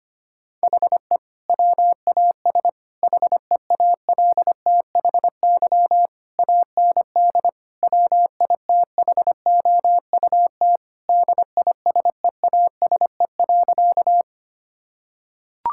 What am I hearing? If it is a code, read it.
Morse code, heard as a keyed tone, about 25 wpm, 700 Hz. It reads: HE WAS HEALTHY AND WITHOUT DISEASE.
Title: Sentences from Top 1000 Words - Review Part 6 - 25wpm